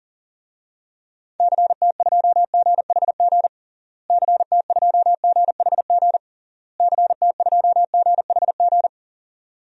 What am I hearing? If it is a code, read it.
Morse code, heard as a keyed tone, about 40 wpm, 700 Hz. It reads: CT2GHG CT2GHG CT2GHG
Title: DXCC-100 Call Signs - 6 Characters - Repeated 3x - 40wpm